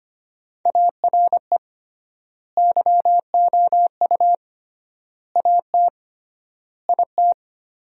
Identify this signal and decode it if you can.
Morse code, heard as a keyed tone, about 25 wpm, 700 Hz. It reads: ARE YOU AT IT
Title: 3x Word-Spacing - Sentences from Top 200 Words - 25wpm